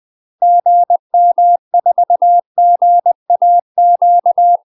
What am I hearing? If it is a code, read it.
Morse code, heard as a keyed tone, about 20 wpm, 700 Hz. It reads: GM4GAQ
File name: 20WPM